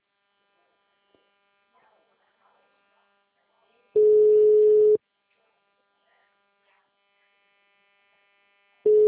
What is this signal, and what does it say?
Ringback tone (Europe, 425 Hz).